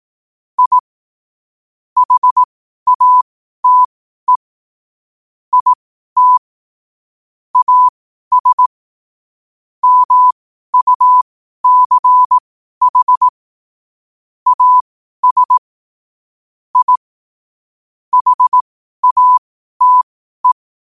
Morse code, heard as a keyed tone, about 18 wpm, spaced as for 12 wpm, 1000 Hz. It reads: I HATE IT AS MUCH AS I HATE
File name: qotdfiles / 12wpm